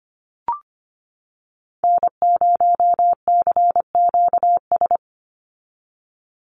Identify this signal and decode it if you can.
Morse code, heard as a keyed tone, about 25 wpm, 700 Hz. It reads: N0CQS
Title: USA Call Signs 25wpm